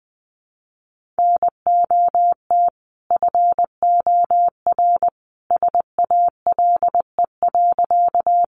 Morse code, heard as a keyed tone, about 20 wpm, 700 Hz. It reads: NOT FOR SALE.
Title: Sentences from Top 1400 Words 20wpm